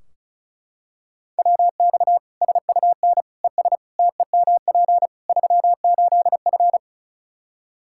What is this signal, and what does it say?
Morse code, heard as a keyed tone, about 36 wpm, 700 Hz. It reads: WX SUN ES TEMP 38F
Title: Practice QSO - Exchange 2 - Info - 35wpm